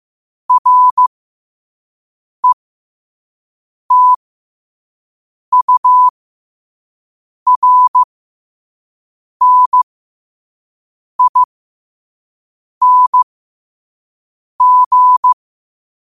Morse code, heard as a keyed tone, about 15 wpm, spaced as for 5 wpm, 1000 Hz. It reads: RETURNING
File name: qotdfiles / 5wpm